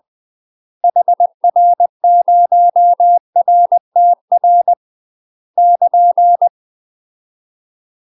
Morse code, heard as a keyed tone, about 20 wpm, 700 Hz. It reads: HR0RTR <KN>
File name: UK MASTER QSOs TRAINING / 20WPM / UK MASTER QSOs 0011